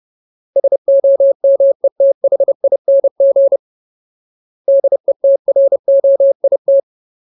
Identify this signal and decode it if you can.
Morse code, heard as a keyed tone, about 30 wpm, 550 Hz. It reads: SOMETHING DETROIT
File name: Ultimate Word List / 30WPM Ultimate